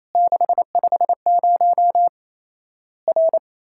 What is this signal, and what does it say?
Morse code, heard as a keyed tone, about 28 wpm, 700 Hz. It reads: 650 R
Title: Worked All Europe DX Contest - QTC Exchange - 28wpm